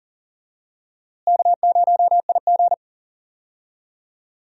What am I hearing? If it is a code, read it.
Morse code, heard as a keyed tone, about 40 wpm, 700 Hz. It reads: K0IG
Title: USA Call Signs - 4 Characters - Repeated 2x - 40wpm